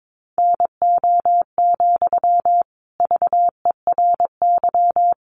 Morse code, heard as a keyed tone, about 22 wpm, 700 Hz. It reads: NO, VERY
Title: Sentences from Top 1500 Words 22wpm